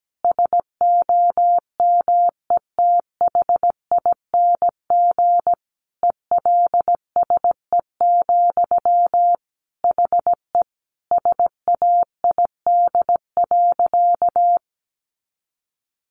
Morse code, heard as a keyed tone, about 17 wpm, 700 Hz. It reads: SOMETHING ELSE, HE SAID.
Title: Sentences from Top 800 Words 17wpm